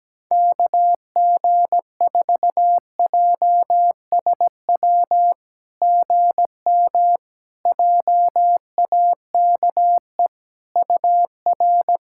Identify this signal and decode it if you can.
Morse code, heard as a keyed tone, about 17 wpm, 690 Hz. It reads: KG4JSW GM JAKE UR
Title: SOTA QSO Practice 17wpm